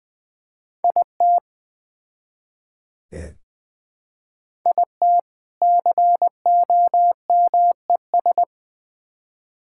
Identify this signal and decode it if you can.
Morse code, heard as a keyed tone, about 20 wpm, 700 Hz. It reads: IT IT COMES
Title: Build A Phrase - 3-Word Phrases - 20wpm